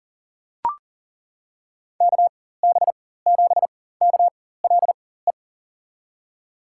Morse code, heard as a keyed tone, about 39 wpm, spaced as for 19 wpm, 700 Hz. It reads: KB7KLE
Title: USA Call Signs - 4x Character Spacing - 40wpm